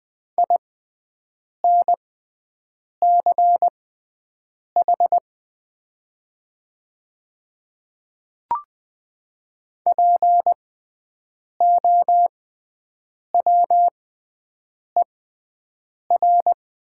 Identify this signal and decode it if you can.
Morse code, heard as a keyed tone, about 20 wpm, spaced as for 7 wpm, 700 Hz. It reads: INCH POWER
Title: 6x Character Spacing - Top 500 Words - 20wpm